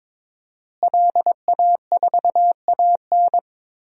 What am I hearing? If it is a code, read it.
Morse code, heard as a keyed tone, about 22 wpm, 700 Hz. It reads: LA4AN
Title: DXCC-100 Call Signs - 5 Characters - Repeated 2x - 22wpm